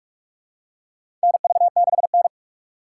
Morse code, heard as a keyed tone, about 46 wpm, 700 Hz. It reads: NV6N
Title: USA Call Signs - 3x Character Spacing - 45wpm